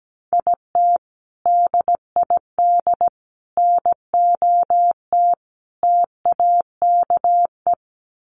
Morse code, heard as a keyed tone, about 17 wpm, 700 Hz. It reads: IT DID NOT TAKE
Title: Sets of 5 Words Encore2 17wpm